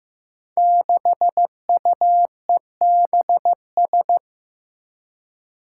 Morse code, heard as a keyed tone, about 15 wpm, 700 Hz. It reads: 6UEBS
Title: Five Challenging Letter-Number Combinations 15wpm